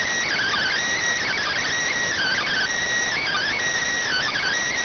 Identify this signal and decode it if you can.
Emergency Alert System (SAME) tones (header bursts), heard as harsh buzzing bursts. It is heard from 0.2 to 4.6 s.